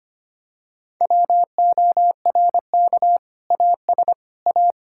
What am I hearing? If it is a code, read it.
Morse code, heard as a keyed tone, about 25 wpm, 700 Hz. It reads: WORK AS A
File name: practice / 25wpm / Sentences with 2-Word Phrases - 25wpm